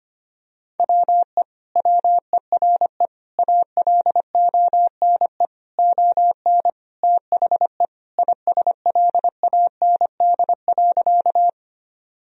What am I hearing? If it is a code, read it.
Morse code, heard as a keyed tone, about 25 wpm, 700 Hz. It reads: WE WERE ALONE ON THE ISLAND.
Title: Sentences from Top 1000 Words - Review Part 3 - 25wpm